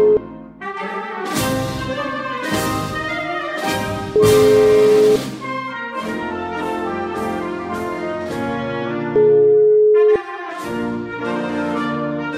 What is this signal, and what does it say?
Ringback tone (Europe, 425 Hz), with noise.